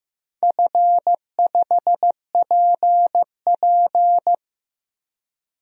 Morse code, heard as a keyed tone, about 15 wpm, 700 Hz. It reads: F5PP
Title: DXCC-100 Call Signs - 4 Characters - 15wpm